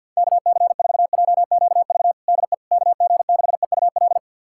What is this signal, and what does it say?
Morse code, heard as a keyed tone, about 52 wpm, 700 Hz. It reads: KC4JQV DE KG6EUL
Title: USA Callsign DE Callsign - 6 Characters - 50wpm